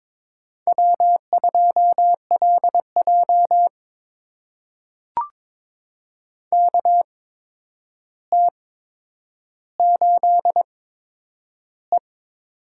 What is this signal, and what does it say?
Morse code, heard as a keyed tone, about 22 wpm, spaced as for 9 wpm, 700 Hz. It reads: W2LJ KT8E
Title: USA Call Signs - 4 Characters - 8x Character Spacing - 22wpm